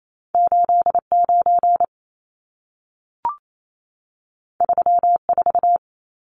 Morse code, heard as a keyed tone, about 28 wpm, 700 Hz. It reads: 89 34
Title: Numbers - 2-Digit 28wpm